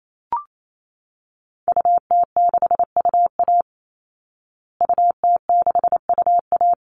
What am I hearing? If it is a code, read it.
Morse code, heard as a keyed tone, about 28 wpm, 700 Hz. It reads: UT6UA UT6UA
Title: Call Signs Repeated 2x - 28wpm